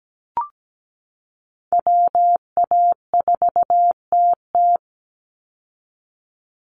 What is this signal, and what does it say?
Morse code, heard as a keyed tone, about 17 wpm, 700 Hz. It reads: WA4TT
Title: Call Signs Repeated 2x - 17wpm